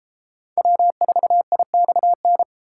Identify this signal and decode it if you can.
Morse code, heard as a keyed tone, about 33 wpm, 700 Hz. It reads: W4IXN